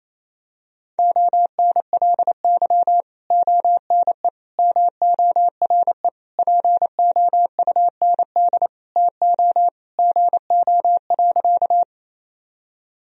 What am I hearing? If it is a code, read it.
Morse code, heard as a keyed tone, about 28 wpm, 700 Hz. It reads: ONLY ONE MORE POUND TO GO.
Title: Sentences from Top 1000 Words - Review Part 2 - 28wpm